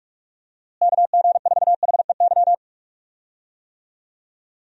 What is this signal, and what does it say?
Morse code, heard as a keyed tone, about 41 wpm, 700 Hz. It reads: KG4HEY
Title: USA Call Signs - 6 Characters - Repeated 3x - 45wpm